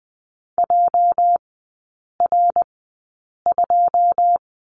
Morse code, heard as a keyed tone, about 20 wpm, spaced as for 8 wpm, 700 Hz. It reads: JR2
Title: 205 Lesson 33e - Letter j - Three Characters 20wpm